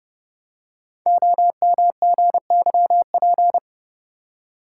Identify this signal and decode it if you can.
Morse code, heard as a keyed tone, about 30 wpm, 690 Hz. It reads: OMGYP
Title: Code Groups - 5 Letters - 30wpm